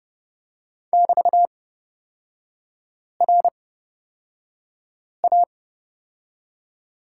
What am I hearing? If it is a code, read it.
Morse code, heard as a keyed tone, about 30 wpm, spaced as for 5 wpm, 700 Hz. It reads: <BT>RA
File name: Lesson 36e - BT - Three Characters 30wpm